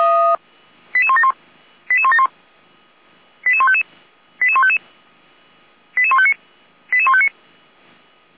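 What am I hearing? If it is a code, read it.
Five-tone selective call (ZVEI), heard as quick runs of five tones, tones of about 70 ms. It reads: 88171 88171 88155 88155 88159 88159
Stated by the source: ZVEI five-tone call received on an SDR, from audio.f85360.feuerwehr-orlamuende.org (Feuerwehr Orlamünde fire service).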